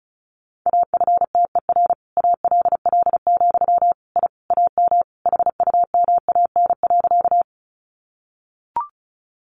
Morse code, heard as a keyed tone, about 34 wpm, 700 Hz. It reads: AFTER ALL, I AM HUMAN.
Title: Sentences from Top 600 Words 35wpm